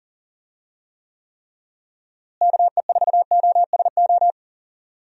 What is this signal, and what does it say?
Morse code, heard as a keyed tone, about 40 wpm, 700 Hz. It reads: KE4OSO
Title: USA Call Signs - 6 Characters - Repeated 3x - 40wpm